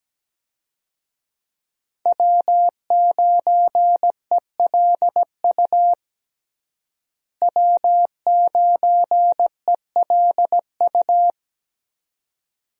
Morse code, heard as a keyed tone, about 17 wpm, 700 Hz. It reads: W9ELU W9ELU
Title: USA Call Signs - 5 Characters - Repeated 2x - 17wpm